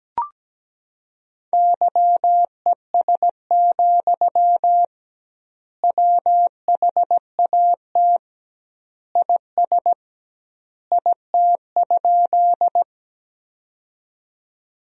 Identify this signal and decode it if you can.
Morse code, heard as a keyed tone, about 17 wpm, 700 Hz. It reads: YES, WHAT IS IT?
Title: 2x Word-Spacing - Sentences from Top 500 Words - 17wpm